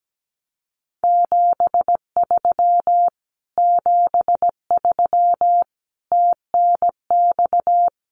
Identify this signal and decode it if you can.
Morse code, heard as a keyed tone, about 17 wpm, 700 Hz. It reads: 73 73 TNX
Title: Common QSO Phrases Encore 17wpm